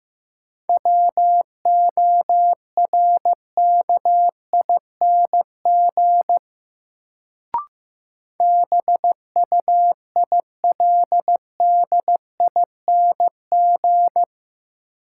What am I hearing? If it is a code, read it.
Morse code, heard as a keyed tone, about 15 wpm, 700 Hz. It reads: WORKING BUILDING
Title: Sentences from Top 700 Words - New Word Drill - 15wpm